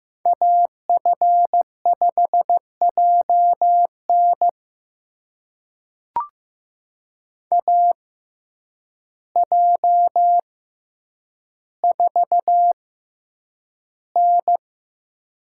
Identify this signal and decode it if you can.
Morse code, heard as a keyed tone, about 15 wpm, spaced as for 8 wpm, 700 Hz. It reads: AF5JN AJ4N
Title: USA Call Signs - 6x Character Spacing - 15wpm